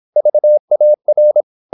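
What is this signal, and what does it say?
Morse code, heard as a keyed tone, about 26 wpm, 600 Hz. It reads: VAR
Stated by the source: High-speed (QRQ) Morse at 26 wpm.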